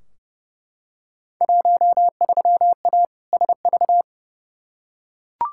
Morse code, heard as a keyed tone, about 30 wpm, 700 Hz. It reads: 13A SV
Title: Field Day Exchanges Encore - 30wpm